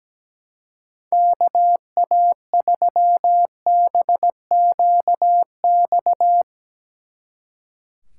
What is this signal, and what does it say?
Morse code, heard as a keyed tone, about 18 wpm, 700 Hz. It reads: KA3BQX
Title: Speed Racing - USA 2x3 Call Signs 17wpm